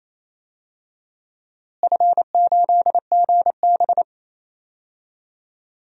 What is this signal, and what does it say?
Morse code, heard as a keyed tone, about 28 wpm, 700 Hz. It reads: F8GB
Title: DXCC-100 Call Signs - 4 Characters - Repeated 3x - 28wpm